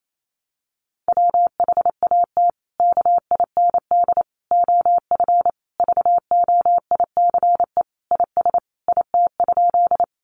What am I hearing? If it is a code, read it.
Morse code, heard as a keyed tone, about 28 wpm, 700 Hz. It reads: WHAT KIND OF VOICE IS IT?
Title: Sentences from Top 500 Words 28wpm